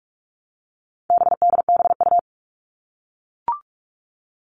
Morse code, heard as a keyed tone, about 44 wpm, 700 Hz. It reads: BDBU
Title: All Cut Numbers - 4-Digit 45wpm